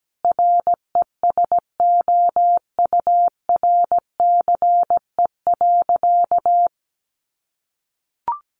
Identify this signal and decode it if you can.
Morse code, heard as a keyed tone, about 17 wpm, 700 Hz. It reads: RESOURCE.
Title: Sentences from Top 1300 Words 17wpm